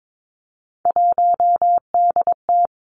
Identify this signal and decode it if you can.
Morse code, heard as a keyed tone, about 22 wpm, 700 Hz. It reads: 1DT
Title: Three Challenging Letter-Number Combinations 22wpm